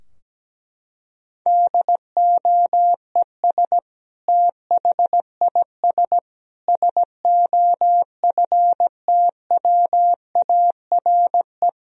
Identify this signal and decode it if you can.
Morse code, heard as a keyed tone, about 17 wpm, 700 Hz. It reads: DOES THIS SOFTWARE
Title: Sentences from Top 1000 Words - Review Part 3 - 17wpm